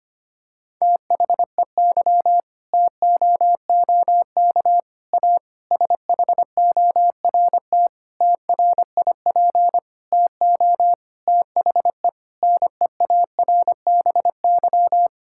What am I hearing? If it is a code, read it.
Morse code, heard as a keyed tone, about 25 wpm, 700 Hz. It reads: THEY TOOK A SHORT TRIP TO THE NEARBY
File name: practice / 25wpm / Difficult Sentences with Hints - 25wpm